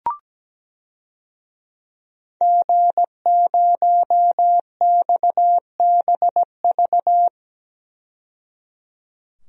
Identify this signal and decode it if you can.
Morse code, heard as a keyed tone, about 17 wpm, 700 Hz. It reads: G0XBV
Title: DXCC-100 Call Signs - 5 Characters - 17wpm